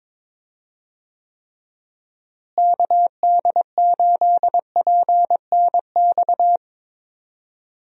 Morse code, heard as a keyed tone, about 22 wpm, 700 Hz. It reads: KD8PNX
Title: USA Call Signs - 6 Characters - Repeated 3x - 22wpm